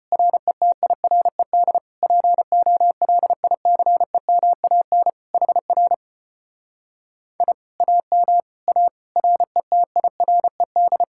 Morse code, heard as a keyed tone, about 33 wpm, 700 Hz. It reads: RETIRED POLICEMAN HR I AM A RETIRED